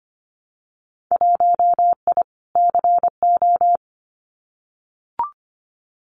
Morse code, heard as a keyed tone, about 25 wpm, 700 Hz. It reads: 1I CO
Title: Winter Field Day Exchanges Encore 25wpm